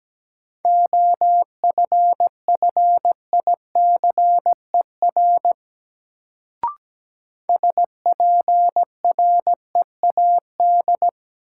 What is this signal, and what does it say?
Morse code, heard as a keyed tone, about 17 wpm, 700 Hz. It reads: OFFICER SPREAD
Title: Sentences from Top 1200 Words - New Word Drill - 17wpm